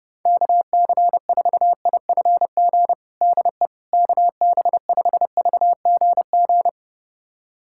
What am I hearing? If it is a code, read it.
Morse code, heard as a keyed tone, about 32 wpm, 700 Hz. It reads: KC4IFG DE KB5VGG
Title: USA Callsign DE Callsign - 6 Characters - 30wpm